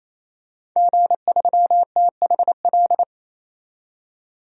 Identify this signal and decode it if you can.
Morse code, heard as a keyed tone, about 28 wpm, 700 Hz. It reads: G3THL
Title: DXCC-100 Call Signs - 5 Characters - 28wpm